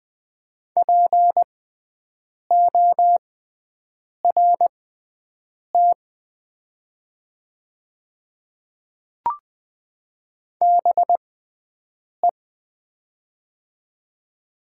Morse code, heard as a keyed tone, about 20 wpm, spaced as for 7 wpm, 700 Hz. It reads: PORT BE
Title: 6x Character Spacing - Top 200 Words - 20wpm